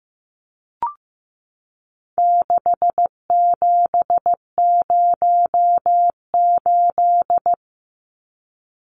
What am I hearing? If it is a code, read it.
Morse code, heard as a keyed tone, about 15 wpm, 700 Hz. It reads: 6708
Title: Numbers - 4-Digit 15wpm